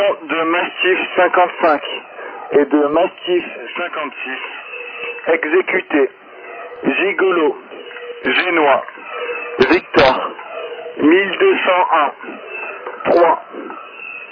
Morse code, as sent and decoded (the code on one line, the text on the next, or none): none